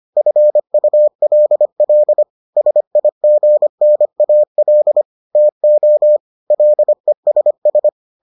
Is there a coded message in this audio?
Yes: Morse code from 0.2 to 7.9 s, heard as a keyed tone, about 25 wpm, 600 Hz. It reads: FULL SIGNAL TO LESS